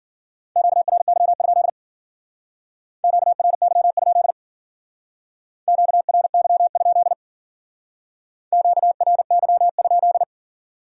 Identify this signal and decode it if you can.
Morse code, heard as a keyed tone, about 49 wpm, 700 Hz. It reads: QRY? QRY? QRY? QRY?
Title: Speed-Racing Q-Codes - 40wpm